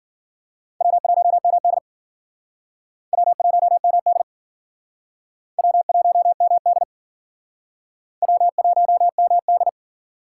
Morse code, heard as a keyed tone, about 52 wpm, 700 Hz. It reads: W1MD W1MD W1MD W1MD